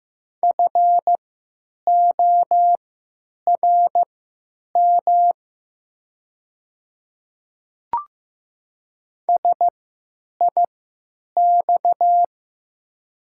Morse code, heard as a keyed tone, about 16 wpm, spaced as for 9 wpm, 700 Hz. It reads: FORM SIX
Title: 3x Character Spacing - Top 500 Words - 15wpm